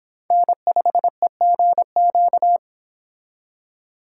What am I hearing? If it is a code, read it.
Morse code, heard as a keyed tone, about 26 wpm, 700 Hz. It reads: N5EGQ